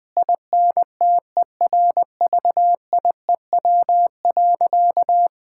Morse code, heard as a keyed tone, about 20 wpm, 700 Hz. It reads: INTERVIEW.
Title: Sentences from Top 1600 Words 20wpm